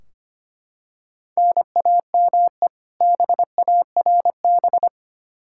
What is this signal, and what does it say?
Morse code, heard as a keyed tone, about 25 wpm, 700 Hz. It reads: NAME BARB
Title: K1USN SST QSO Practice 25wpm